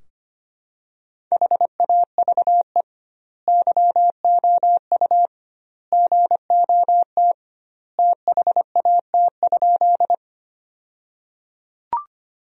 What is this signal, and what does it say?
Morse code, heard as a keyed tone, about 25 wpm, 700 Hz. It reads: HAVE YOU GOT THAT?
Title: 2x Word-Spacing - Sentences from Top 300 Words - 25wpm